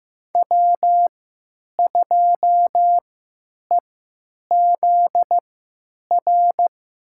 Morse code, heard as a keyed tone, about 15 wpm, spaced as for 9 wpm, 700 Hz. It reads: W2EZR
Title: USA Call Signs - 5 Characters - 3x Character Spacing - 15wpm